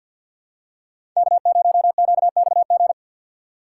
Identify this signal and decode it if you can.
Morse code, heard as a keyed tone, about 47 wpm, 700 Hz. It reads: K0QXG
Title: USA Call Signs - 5 Characters - 4x Character Spacing - 50wpm